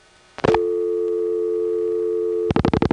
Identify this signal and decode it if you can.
Dial tone (North America), 350 + 440 Hz.